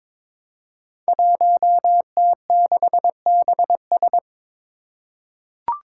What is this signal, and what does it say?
Morse code, heard as a keyed tone, about 22 wpm, 700 Hz. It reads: 1T6BS
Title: Five Challenging Letter-Number Combinations 22wpm